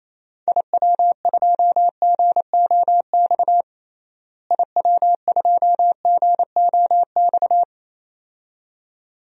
Morse code, heard as a keyed tone, about 27 wpm, 700 Hz. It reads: IW2GOX IW2GOX
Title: DXCC-100 Call Signs - 6 Characters - Repeated 2x - 28wpm